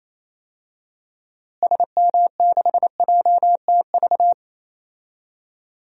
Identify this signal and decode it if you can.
Morse code, heard as a keyed tone, about 28 wpm, 700 Hz. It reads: SM6JTV
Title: DXCC-100 Call Signs - 6 Characters - Repeated 2x - 28wpm